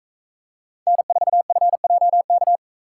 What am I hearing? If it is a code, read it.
Morse code, heard as a keyed tone, about 42 wpm, 700 Hz. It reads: N4FJK